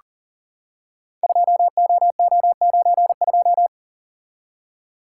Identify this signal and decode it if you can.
Morse code, heard as a keyed tone, about 40 wpm, 700 Hz. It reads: 2OO92
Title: Code Groups - 5 Letter:Numbers - 40wpm